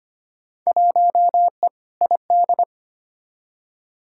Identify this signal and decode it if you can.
Morse code, heard as a keyed tone, about 25 wpm, 700 Hz. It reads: 1E ID
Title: Field Day Exchanges - 25wpm